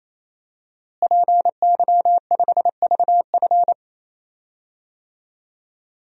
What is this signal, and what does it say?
Morse code, heard as a keyed tone, about 28 wpm, 700 Hz. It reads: PY5VF
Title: DXCC-100 Call Signs - 5 Characters - Repeated 3x - 28wpm